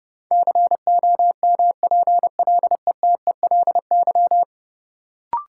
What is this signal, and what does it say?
Morse code, heard as a keyed tone, about 32 wpm, 700 Hz. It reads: COMPLETELY
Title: Sentences from Top 1100 Words - New Word Drill - 30wpm